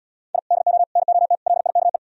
Morse code, heard as a keyed tone, about 77 wpm, 700 Hz. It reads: I DO NOT LIKE